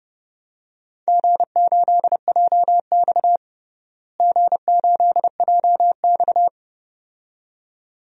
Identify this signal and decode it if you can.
Morse code, heard as a keyed tone, about 32 wpm, 700 Hz. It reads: G8JX G8JX
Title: DXCC-100 Call Signs - 4 Characters - Repeated 2x - 30wpm